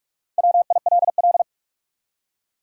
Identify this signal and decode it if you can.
Morse code, heard as a keyed tone, about 46 wpm, 700 Hz. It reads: WILL
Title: USA Common Names - 45wpm